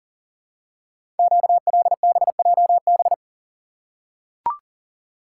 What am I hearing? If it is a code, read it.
Morse code, heard as a keyed tone, about 40 wpm, 700 Hz. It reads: QLBJB
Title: Code Groups - 5 Letters - 40wpm